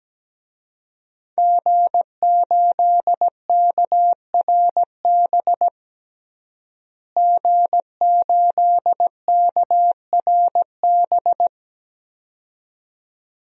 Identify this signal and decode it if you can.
Morse code, heard as a keyed tone, about 17 wpm, 700 Hz. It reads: G8KRB G8KRB
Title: DXCC-100 Call Signs - 5 Characters - Repeated 2x - 17wpm